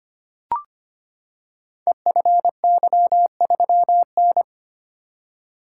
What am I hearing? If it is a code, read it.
Morse code, heard as a keyed tone, about 25 wpm, 700 Hz. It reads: EFY3N
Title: Code Groups - 5 Letter:Numbers - 25wpm